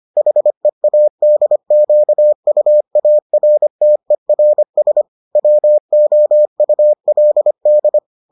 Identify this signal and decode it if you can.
Morse code, heard as a keyed tone, about 25 wpm, 600 Hz. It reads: HEADQUARTERS WOULD